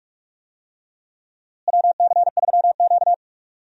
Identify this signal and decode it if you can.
Morse code, heard as a keyed tone, about 46 wpm, 700 Hz. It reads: WC3Q
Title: USA Call Signs - 4 Characters - Repeated 2x - 45wpm